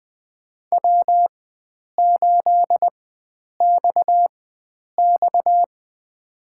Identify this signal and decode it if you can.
Morse code, heard as a keyed tone, about 20 wpm, spaced as for 9 wpm, 700 Hz. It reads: W8XX
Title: USA Call Signs - 4 Characters - 4x Character Spacing - 20wpm